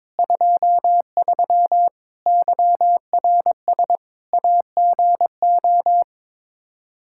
Morse code, heard as a keyed tone, about 22 wpm, 700 Hz. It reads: 23 YRS AGO
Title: K1USN SST QSO Practice 22wpm